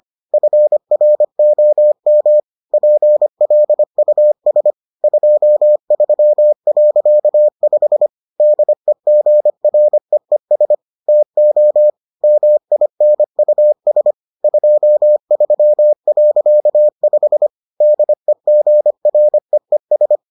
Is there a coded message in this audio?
Yes: Morse code from 0.3 to 20.2 s, heard as a keyed tone, about 25 wpm, 600 Hz. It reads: FROM PLUS 23.5 DEGREES TO MINUS 23.5 DEGREES